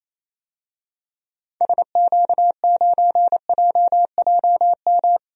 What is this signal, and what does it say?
Morse code, heard as a keyed tone, about 28 wpm, 700 Hz. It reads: SQ9JJM
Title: DXCC-100 Call Signs - 6 Characters - 28wpm